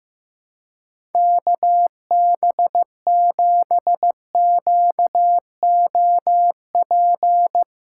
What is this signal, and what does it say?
Morse code, heard as a keyed tone, about 16 wpm, 700 Hz. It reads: KB7QOP
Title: K1USN SST QSO Practice - 15wpm